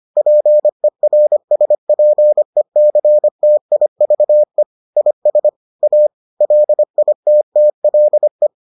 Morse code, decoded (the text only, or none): PERSPECTIVE IS A LITTLE